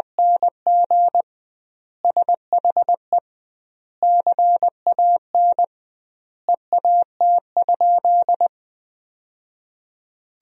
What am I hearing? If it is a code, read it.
Morse code, heard as a keyed tone, about 20 wpm, 700 Hz. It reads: NG SHE CAN EAT?
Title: 2x Word-Spacing - Sentences from Top 500 Words - 20wpm